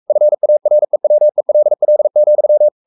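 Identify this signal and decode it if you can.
Morse code, heard as a keyed tone, about 44 wpm, 600 Hz. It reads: FAREWELL,
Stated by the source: Morse code at 43 wpm sending Ken Follett's novel Never in English, from CW med Gnister.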